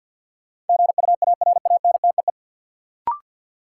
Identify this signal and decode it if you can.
Morse code, heard as a keyed tone, about 47 wpm, 700 Hz. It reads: GUARANTEE
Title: Sentences from Top 1600 Words - New Word Drill - 50wpm